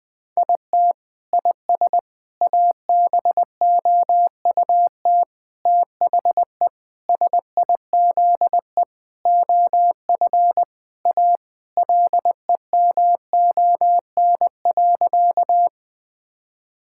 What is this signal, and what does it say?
Morse code, heard as a keyed tone, about 20 wpm, 700 Hz. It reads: IT IS ABOUT THE SIZE OF A LEMON.
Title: Sentences with 5-Word Phrases - Spoken Beginning - 20wpm